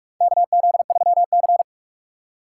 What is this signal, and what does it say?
Morse code, heard as a keyed tone, about 47 wpm, 690 Hz. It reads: KZ3C
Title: USA Call Signs - 4 Characters - 4x Character Spacing - 45wpm